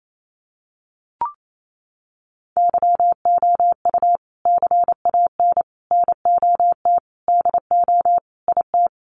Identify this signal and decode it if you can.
Morse code, heard as a keyed tone, about 28 wpm, 700 Hz. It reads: YOU CAN NOT DO IT
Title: Sets of 5 Words Encore2 28wpm